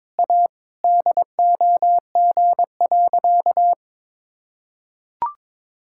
Morse code, heard as a keyed tone, about 22 wpm, 700 Hz. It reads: A DOG.
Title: Sentences from Top 1000 Words - Review Part 5 - 22wpm